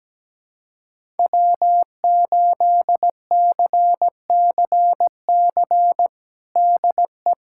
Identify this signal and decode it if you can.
Morse code, heard as a keyed tone, about 17 wpm, 700 Hz. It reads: W8CCC DE
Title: USA Callsign DE Callsign 17wpm